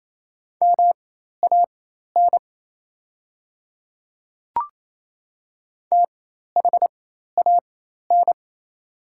Morse code, heard as a keyed tone, about 28 wpm, spaced as for 13 wpm, 700 Hz. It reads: MAN THAN
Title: 4x Character Spacing - Top 200 Words - 28wpm